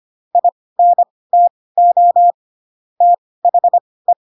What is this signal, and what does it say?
Morse code, heard as a keyed tone, about 25 wpm, spaced as for 18 wpm, 700 Hz. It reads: INTO THE
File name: Pinocchio ch3 12 25 ik1hge